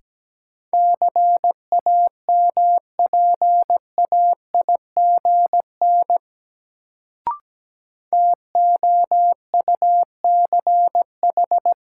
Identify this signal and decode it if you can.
Morse code, heard as a keyed tone, about 17 wpm, 700 Hz. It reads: CAMPAIGN TOUCH
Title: Sentences from Top 1200 Words - New Word Drill - 17wpm